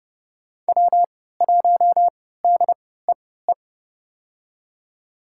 Morse code, heard as a keyed tone, about 31 wpm, spaced as for 17 wpm, 700 Hz. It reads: W1DEE
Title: USA Call Signs - 5 Characters - 3x Character Spacing - 30wpm